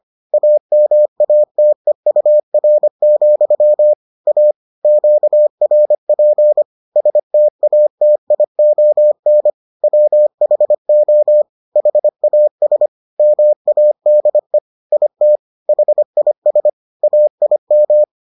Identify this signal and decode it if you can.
Morse code, heard as a keyed tone, about 25 wpm, 600 Hz. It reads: AMATEUR, A QRP STATION WHO HAS MADE IT HIS AIM